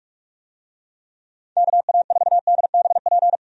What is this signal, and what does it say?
Morse code, heard as a keyed tone, about 44 wpm, 700 Hz. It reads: KA4DBP
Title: USA Call Signs - 6 Characters - 45wpm